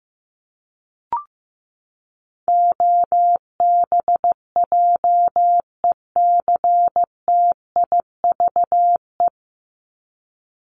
Morse code, heard as a keyed tone, about 15 wpm, 700 Hz. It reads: OBJECTIVE
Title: Top 1001-1250 Words 15wpm